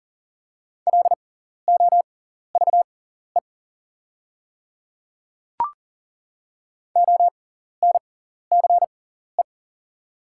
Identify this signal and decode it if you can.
Morse code, heard as a keyed tone, about 40 wpm, spaced as for 14 wpm, 690 Hz. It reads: LOVE ONCE
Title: 6x Character Spacing - Top 500 Words - 40wpm